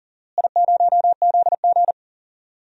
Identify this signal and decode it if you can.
Morse code, heard as a keyed tone, about 40 wpm, 700 Hz. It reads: I0ZG
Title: DXCC-100 Call Signs - 4 Characters - Repeated 3x - 40wpm